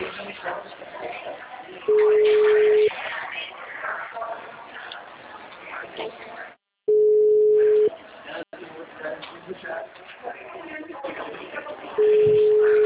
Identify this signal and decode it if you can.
Ringback tone (Europe, 425 Hz), with speech.
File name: out- -2002- - - .2575